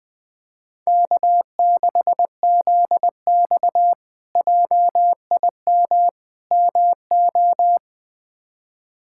Morse code, heard as a keyed tone, about 21 wpm, 700 Hz. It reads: K6ZX JIM MO
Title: K1USN SST QSO Practice - 20wpm